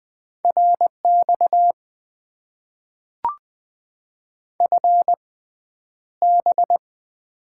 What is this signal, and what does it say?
Morse code, heard as a keyed tone, about 20 wpm, spaced as for 10 wpm, 700 Hz. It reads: RX FB
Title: Top 90 QSO Elements - 6x Character Spacing - 20wpm